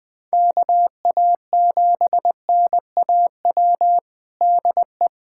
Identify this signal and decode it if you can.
Morse code, heard as a keyed tone, about 19 wpm, 700 Hz. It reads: KA7NAW DE
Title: USA Callsign DE Callsign 20wpm